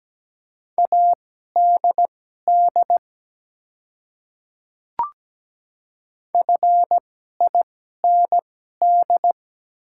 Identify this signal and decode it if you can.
Morse code, heard as a keyed tone, about 17 wpm, spaced as for 12 wpm, 700 Hz. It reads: ADD FIND
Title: 2x Character Spacing - Top 200 Words - 17wpm